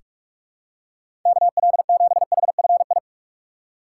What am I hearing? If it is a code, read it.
Morse code, heard as a keyed tone, about 41 wpm, 700 Hz. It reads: KL7HFI
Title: K1USN SST QSO Practice - 45wpm